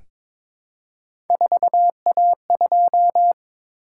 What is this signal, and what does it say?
Morse code, heard as a keyed tone, about 23 wpm, 700 Hz. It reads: 4A2
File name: practice / 22wpm / Three Letter-Number Combinations - 22wpm